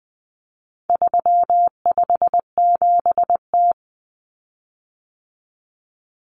Morse code, heard as a keyed tone, about 20 wpm, 700 Hz. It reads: 357T
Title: Common Cut Numbers - 4-Digit 20wpm